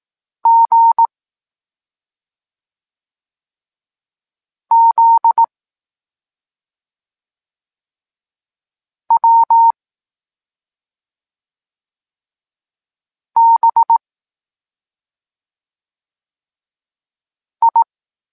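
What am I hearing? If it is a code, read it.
Morse code, heard as a keyed tone, about 18 wpm, 920 Hz. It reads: G Z W B I
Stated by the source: Evenly spaced Morse lesson text 21 for learning letters G and Z, from SP9MOA Niepołomicki Klub Krótkofalowców.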